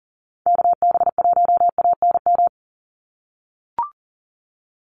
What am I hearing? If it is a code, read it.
Morse code, heard as a keyed tone, about 40 wpm, 700 Hz. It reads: KB1ANM
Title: USA Call Signs 40wpm